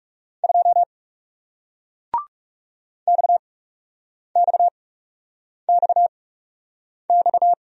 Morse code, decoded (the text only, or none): 2 XXXX